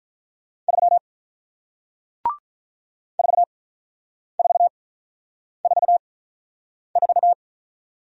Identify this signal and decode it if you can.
Morse code, heard as a keyed tone, about 47 wpm, spaced as for 9 wpm, 700 Hz. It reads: 3 4444